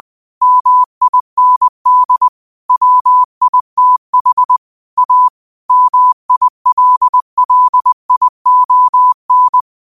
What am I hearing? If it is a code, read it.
Morse code, heard as a keyed tone, about 20 wpm, 1000 Hz. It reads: MIND WITH A MILLION
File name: qotdfiles / 20wpm